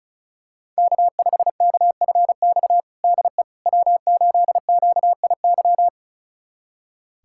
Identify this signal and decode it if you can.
Morse code, heard as a keyed tone, about 36 wpm, 690 Hz. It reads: K5KFX DE W8QIY
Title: USA Callsign DE Callsign - 5 Characters - 35wpm